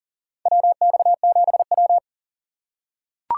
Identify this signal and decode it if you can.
Morse code, heard as a keyed tone, about 40 wpm, 700 Hz. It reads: WX7W
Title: USA Call Signs - 4 Characters - 6x Character Spacing - 40wpm